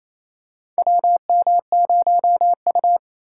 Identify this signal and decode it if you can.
Morse code, heard as a keyed tone, about 28 wpm, 700 Hz. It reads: WM0U
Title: USA Call Signs - 2x Character Spacing - 28wpm